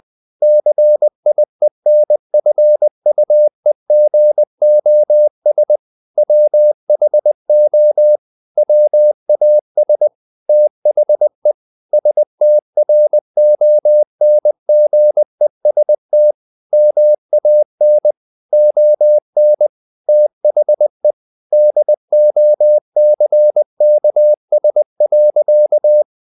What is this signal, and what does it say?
Morse code, heard as a keyed tone, about 20 wpm, 600 Hz. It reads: CIENFUEGOS WHO WAS THE STRONGEST MAN ON THE DOCKS.
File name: the old man and the sea / the old man and the sea part 1 20wpm